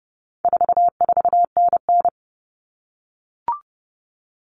Morse code, heard as a keyed tone, about 30 wpm, 700 Hz. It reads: 44NN
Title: Common Cut Numbers - 4-Digit 30wpm